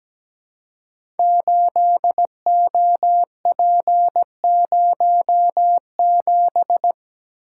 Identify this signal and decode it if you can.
Morse code, heard as a keyed tone, about 17 wpm, 700 Hz. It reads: 8OP07
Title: Code Groups - 5 Letter:Numbers - 17wpm